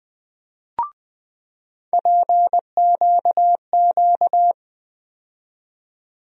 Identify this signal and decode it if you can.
Morse code, heard as a keyed tone, about 20 wpm, 700 Hz. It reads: PQQ
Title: USA Call Sign Postfixes - 3 Letters 20wpm